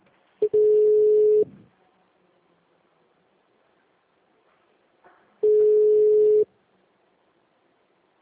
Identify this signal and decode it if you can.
Ringback tone (Europe, 425 Hz).